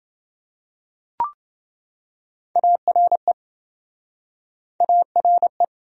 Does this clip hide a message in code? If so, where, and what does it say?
Yes: Morse code from 2.6 to 5.6 s, heard as a keyed tone, about 26 wpm, 700 Hz. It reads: ARE ARE